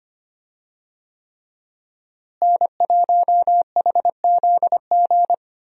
Morse code, heard as a keyed tone, about 25 wpm, 700 Hz. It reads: N1HZG
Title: USA Call Signs - 5 Characters - Repeated 3x - 25wpm